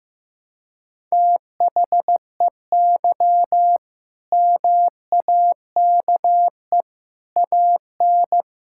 Morse code, heard as a keyed tone, about 14 wpm, 700 Hz. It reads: THEY MAKE AN
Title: Sentences from Top 1000 Words - Review Part 6 - 15wpm